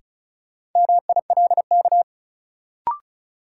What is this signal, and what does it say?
Morse code, heard as a keyed tone, about 34 wpm, 700 Hz. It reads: MILK
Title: Sentences from Top 1500 Words - New Word Drill - 35wpm